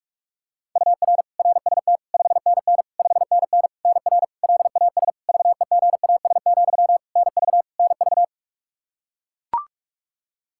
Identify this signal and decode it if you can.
Morse code, heard as a keyed tone, about 43 wpm, 700 Hz. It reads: UR RST 5NN 5NN NR LAS VEGAS, NV NV
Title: K1USN SST QSO Practice 45wpm